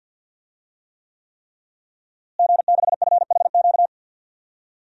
Morse code, heard as a keyed tone, about 52 wpm, 700 Hz. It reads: G6FHQ
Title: DXCC-100 Call Signs - 5 Characters - Repeated 3x - 50wpm